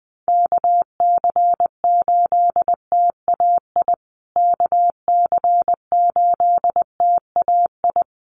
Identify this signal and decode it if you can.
Morse code, heard as a keyed tone, about 20 wpm, 700 Hz. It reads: KC8TAI KC8TAI
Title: USA Call Signs Repeated 3x 20wpm